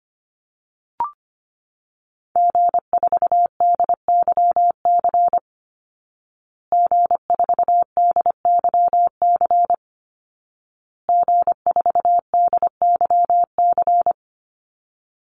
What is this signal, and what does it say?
Morse code, heard as a keyed tone, about 25 wpm, 700 Hz. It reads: G4DYC G4DYC G4DYC
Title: Call Signs Repeated 2x - 25wpm